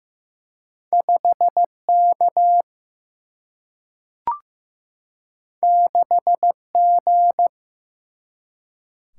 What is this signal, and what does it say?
Morse code, heard as a keyed tone, about 15 wpm, 700 Hz. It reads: 5 K 6 G
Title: Single-Letter Single-Number 15wpm